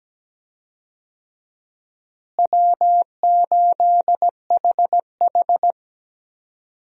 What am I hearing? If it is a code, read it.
Morse code, heard as a keyed tone, about 17 wpm, 700 Hz. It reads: W8HH
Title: USA Call Signs - 4 Characters - Repeated 3x - 17wpm